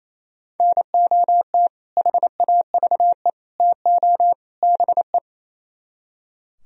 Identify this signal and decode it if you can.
Morse code, heard as a keyed tone, about 35 wpm, 700 Hz. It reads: NOT HAVE TO BE